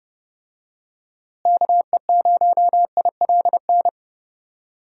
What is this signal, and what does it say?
Morse code, heard as a keyed tone, about 32 wpm, 700 Hz. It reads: KE0ILN
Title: USA Call Signs - 6 Characters - Repeated 2x - 30wpm